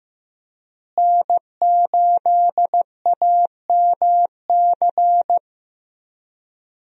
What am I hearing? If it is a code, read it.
Morse code, heard as a keyed tone, about 15 wpm, 700 Hz. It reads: N8AMC
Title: USA Call Signs - 5 Characters - Repeated 3x - 15wpm